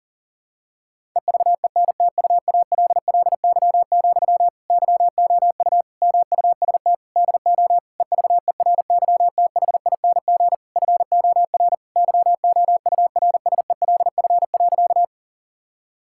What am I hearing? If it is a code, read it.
Morse code, heard as a keyed tone, about 40 wpm, 700 Hz. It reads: EVENTUALLY, YOU MUST DO EVERYTHING FOR YOURSELF.
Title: Sentences from Top 1000 Words - Review Part 6 - 40wpm